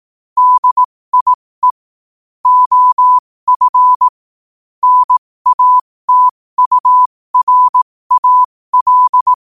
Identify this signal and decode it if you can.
Morse code, heard as a keyed tone, about 18 wpm, spaced as for 15 wpm, 1000 Hz. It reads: DIE OF NATURAL
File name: qotdfiles / 15wpm